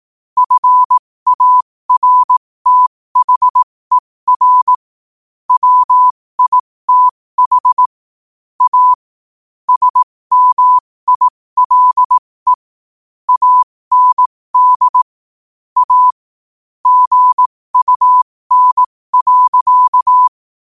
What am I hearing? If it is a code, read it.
Morse code, heard as a keyed tone, about 18 wpm, spaced as for 15 wpm, 1000 Hz. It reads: FARTHER WITH A SMILE AND A GUN.
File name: qotdfiles / 15wpm